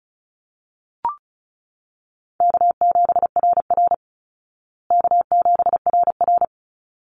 Morse code, heard as a keyed tone, about 34 wpm, 700 Hz. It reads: K7RR K7RR
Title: Call Signs Repeated 3x - 35wpm